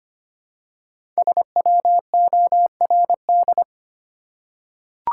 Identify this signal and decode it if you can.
Morse code, heard as a keyed tone, about 25 wpm, 700 Hz. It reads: SWORD
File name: 5 Letter Words - First 3 Letters Given - 25wpm